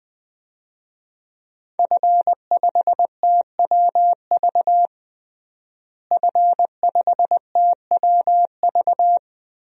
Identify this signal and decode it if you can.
Morse code, heard as a keyed tone, about 21 wpm, 700 Hz. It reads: F5TWV F5TWV
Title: DXCC-100 Call Signs - 5 Characters - Repeated 3x - 20wpm